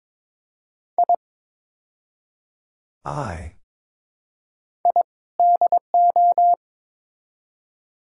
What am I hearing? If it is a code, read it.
Morse code, heard as a keyed tone, about 22 wpm, 700 Hz. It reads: I I DO
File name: practice / 22wpm / Build A Phrase - 5-Word Phrases - 22wpm